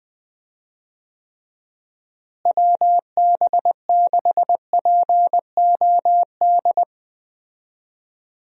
Morse code, heard as a keyed tone, about 20 wpm, 700 Hz. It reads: WB6POD
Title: USA Call Signs - 6 Characters - Repeated 3x - 20wpm